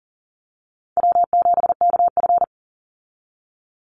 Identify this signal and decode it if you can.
Morse code, heard as a keyed tone, about 40 wpm, 700 Hz. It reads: W7KF
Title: Call Signs - 40wpm